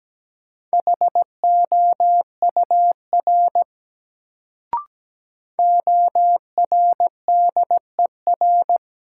Morse code, heard as a keyed tone, about 17 wpm, 700 Hz. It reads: HOUR ORDER
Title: Sentences from Top 400 Words - New Word Drill - 17wpm